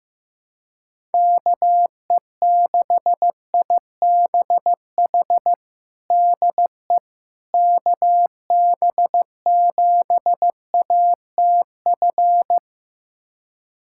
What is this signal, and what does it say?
Morse code, heard as a keyed tone, about 16 wpm, 700 Hz. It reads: KE6IBH DE KB7ATF
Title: USA Callsign DE Callsign - 6 Characters - 15wpm